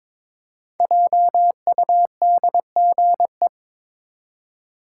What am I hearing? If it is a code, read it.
Morse code, heard as a keyed tone, about 23 wpm, 700 Hz. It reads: JUDGE
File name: 5 Letter Words - First 3 Letters Given - 22wpm